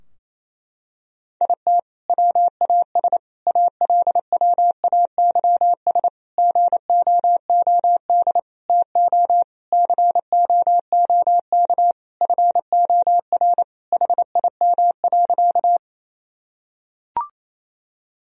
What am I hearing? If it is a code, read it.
Morse code, heard as a keyed tone, about 28 wpm, 700 Hz. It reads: IT WAS ALWAYS GOOD TO COOK FOR HIM.
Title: Sentences from Top 1000 Words - Review Part 4 - 28wpm